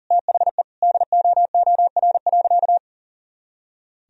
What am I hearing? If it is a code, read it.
Morse code, heard as a keyed tone, about 40 wpm, 700 Hz. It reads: THE DOOR.
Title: Sentences from Top 400 Words and Abbreviations 40wpm